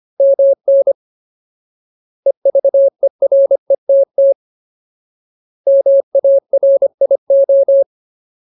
Morse code, heard as a keyed tone, about 25 wpm, 550 Hz. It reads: MN EVERETT MARIO
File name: Ultimate Word List / 25WPM Ultimate